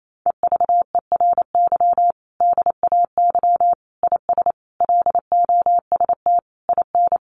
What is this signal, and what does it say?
Morse code, heard as a keyed tone, about 28 wpm, 700 Hz. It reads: EVERY DAY IS LOST IN
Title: Sentences from Top 1400 Words 28wpm